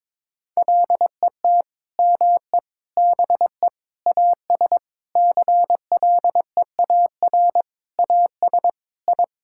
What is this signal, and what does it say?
Morse code, heard as a keyed tone, about 22 wpm, 700 Hz. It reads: LET ME BE AS CLEAR AS I
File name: Sentences with 9-Word Phrases - 22wpm